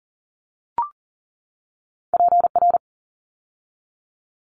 Morse code, heard as a keyed tone, about 42 wpm, 700 Hz. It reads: PR
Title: ARRL Sections 40wpm